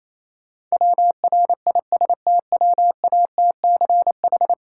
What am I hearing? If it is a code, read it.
Morse code, heard as a keyed tone, about 27 wpm, 690 Hz. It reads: WRISTWATCH
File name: Easy Words - Speed-Racing with Hints - 22wpm